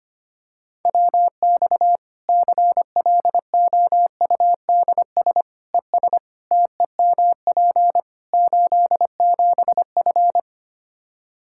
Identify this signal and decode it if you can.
Morse code, heard as a keyed tone, about 25 wpm, 700 Hz. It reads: WX CLOUDS ES TEMP 87F
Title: Practice QSO - Exchange 2 - Info - 25wpm